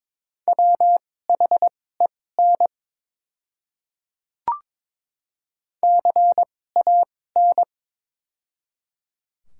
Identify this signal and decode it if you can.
Morse code, heard as a keyed tone, about 22 wpm, spaced as for 16 wpm, 700 Hz. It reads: WHEN CAN
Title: 2x Character Spacing - Top 100 Words - 22wpm